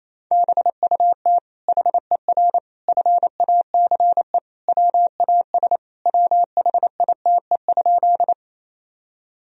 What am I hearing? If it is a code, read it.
Morse code, heard as a keyed tone, about 28 wpm, 700 Hz. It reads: BUT HER FACE WAS WHITE?
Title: Sentences from Top 1000 Words - Review Part 2 - 28wpm